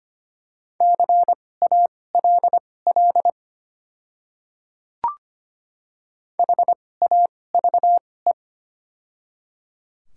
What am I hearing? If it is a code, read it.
Morse code, heard as a keyed tone, about 25 wpm, spaced as for 18 wpm, 700 Hz. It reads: CALL HAVE
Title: 2x Character Spacing - Top 500 Words - 25wpm